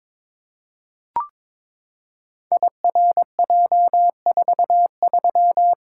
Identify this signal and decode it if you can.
Morse code, heard as a keyed tone, about 22 wpm, 700 Hz. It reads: IRJ43
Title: Code Groups - 5 Letter:Numbers - 22wpm